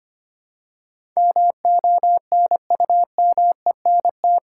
Morse code, heard as a keyed tone, about 25 wpm, 690 Hz. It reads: MONUMENT
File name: 8 Letter Words - First 3 Letters Given - 25wpm